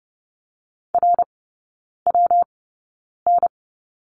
Morse code, heard as a keyed tone, about 30 wpm, spaced as for 9 wpm, 700 Hz. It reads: RWN
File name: Lesson 19e - Letter w - Three Characters 30wpm